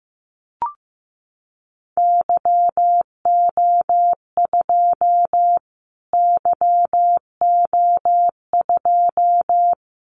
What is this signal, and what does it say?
Morse code, heard as a keyed tone, about 15 wpm, 700 Hz. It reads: YO2 YO2
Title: Callsign Prefixes for Top 25 DXCC - Repeated 3x - 15wpm